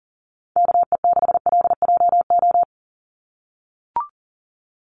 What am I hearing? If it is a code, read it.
Morse code, heard as a keyed tone, about 37 wpm, 700 Hz. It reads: KE6LJO
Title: USA Call Signs 40wpm